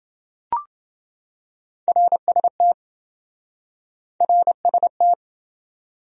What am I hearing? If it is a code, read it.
Morse code, heard as a keyed tone, about 26 wpm, 700 Hz. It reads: RST RST